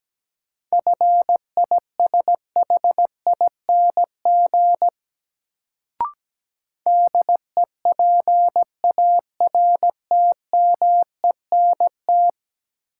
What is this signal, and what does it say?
Morse code, heard as a keyed tone, about 17 wpm, 700 Hz. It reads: FISHING DEPARTMENT
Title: Sentences from Top 1000 Words - Review Part 5 - New Word Drill - 17wpm